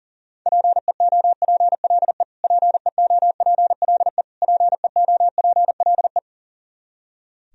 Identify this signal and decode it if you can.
Morse code, heard as a keyed tone, about 40 wpm, 700 Hz. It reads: PEOPLE PEOPLE PEOPLE
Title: Top 100 Words Repeated 40wpm